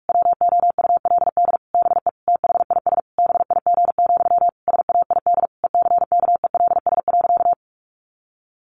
Morse code, heard as a keyed tone, about 44 wpm, 700 Hz. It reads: WOULD BE THIS BIG, SAID ECKELS.
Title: A Sound Of Thunder 45wpm